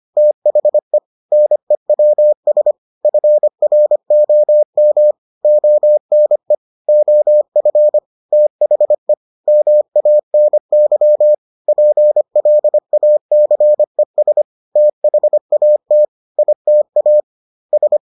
Morse code, heard as a keyed tone, about 25 wpm, 600 Hz. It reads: THE NEWS FROM ONE OF THE MANY PLACES THAT ITA S